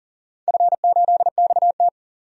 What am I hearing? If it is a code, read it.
Morse code, heard as a keyed tone, about 40 wpm, 700 Hz. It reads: F8XT
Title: DXCC-100 Call Signs - 4 Characters - 40wpm